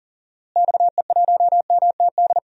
Morse code, heard as a keyed tone, about 37 wpm, 700 Hz. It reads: XE1MTD
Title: DXCC-100 Call Signs - 6 Characters - Repeated 2x - 40wpm